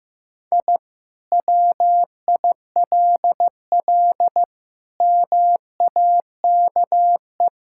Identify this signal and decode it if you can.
Morse code, heard as a keyed tone, about 15 wpm, 700 Hz. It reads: I WILL MAKE
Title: Sentences from Top 1000 Words - Review Part 5 - 15wpm